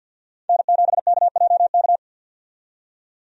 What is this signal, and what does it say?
Morse code, heard as a keyed tone, about 47 wpm, 700 Hz. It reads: N7KJK
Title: USA Call Signs - 5 Characters - 50wpm